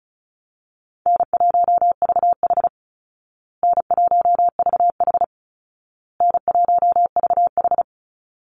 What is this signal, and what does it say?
Morse code, heard as a keyed tone, about 34 wpm, 700 Hz. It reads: N1VH N1VH N1VH
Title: Call Signs Repeated 2x - 35wpm